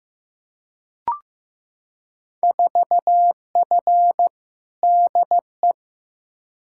Morse code, heard as a keyed tone, about 15 wpm, 700 Hz. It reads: 4F DE
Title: Field Day Exchanges Encore - 15wpm